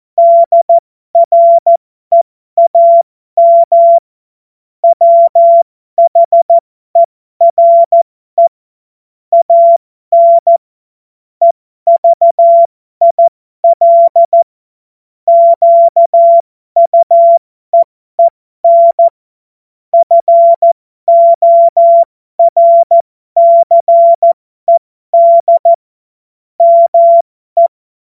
Morse code, heard as a keyed tone, about 14 wpm, spaced as for 12 wpm, 680 Hz. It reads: DREAM WHERE AN EVIL QUEEN FORCED ME
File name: Morsekurs / Witze